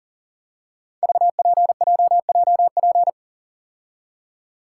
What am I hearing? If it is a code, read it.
Morse code, heard as a keyed tone, about 38 wpm, 700 Hz. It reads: VPJJP
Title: Code Groups - 5 Letters - 40wpm